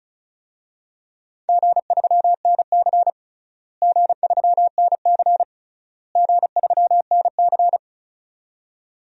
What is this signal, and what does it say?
Morse code, heard as a keyed tone, about 36 wpm, 700 Hz. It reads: G3NC G3NC G3NC
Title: DXCC-100 Call Signs - 4 Characters - Repeated 3x - 35wpm